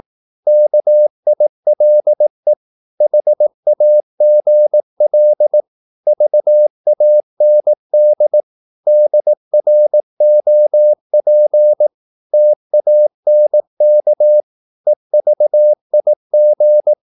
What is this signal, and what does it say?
Morse code, heard as a keyed tone, about 18 wpm, 600 Hz. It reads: KILE HAGL VAND DROP TANK EVIG
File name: Korte ord DK 18wpm